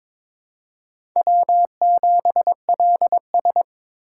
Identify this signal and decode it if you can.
Morse code, heard as a keyed tone, about 23 wpm, 700 Hz. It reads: W7LS
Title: USA Call Signs - 4 Characters - 2x Character Spacing - 22wpm